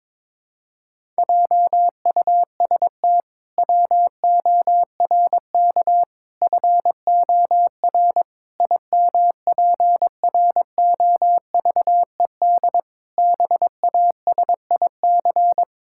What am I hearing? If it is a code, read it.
Morse code, heard as a keyed tone, about 22 wpm, 700 Hz. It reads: JUST WORK FOR IMPROVED BASIC
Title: Pangram Challenge v2 - 22wpm